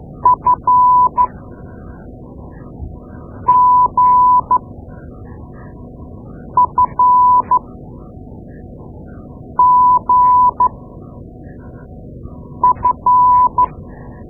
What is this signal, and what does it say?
Morse code, heard as a keyed tone, about 12 wpm, spaced as for 4 wpm, 1000 Hz. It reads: FGFGF
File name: Morse Lessons / Morse Lessons / Tape-2A